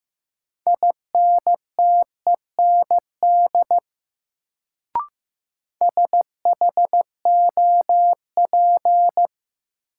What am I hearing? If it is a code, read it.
Morse code, heard as a keyed tone, about 15 wpm, 700 Hz. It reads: INTEND SHOP
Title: Sentences from Top 1600 Words - New Word Drill - 15wpm